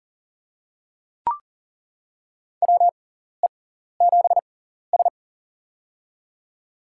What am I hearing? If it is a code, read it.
Morse code, heard as a keyed tone, about 38 wpm, spaced as for 14 wpm, 700 Hz. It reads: WE7S
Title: USA Call Signs - 6x Character Spacing - 40wpm